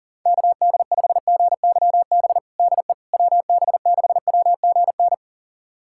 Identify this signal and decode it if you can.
Morse code, heard as a keyed tone, about 40 wpm, 700 Hz. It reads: KD5GYB DE WB6WGN
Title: USA Callsign DE Callsign - 6 Characters - 40wpm